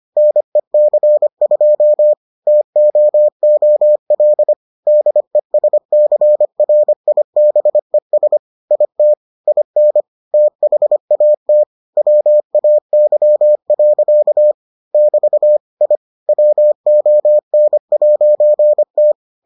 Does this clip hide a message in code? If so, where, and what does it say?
Yes: Morse code from 0.2 to 19.1 s, heard as a keyed tone, about 25 wpm, 600 Hz. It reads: NEC2 TOOL DESCRIBES IT IN THAT WAY. <BT> I WON'T